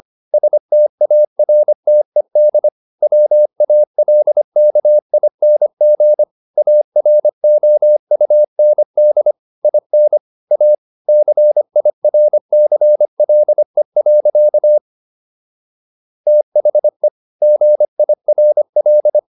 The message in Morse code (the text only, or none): STARTED WALKING AROUND IN A CIRCLE. THE GIRL